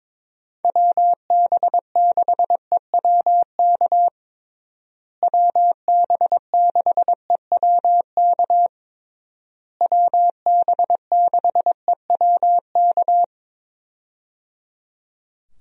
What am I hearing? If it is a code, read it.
Morse code, heard as a keyed tone, about 22 wpm, 700 Hz. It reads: WB6EWK WB6EWK WB6EWK
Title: USA Call Signs - 6 Characters - Repeated 3x - 22wpm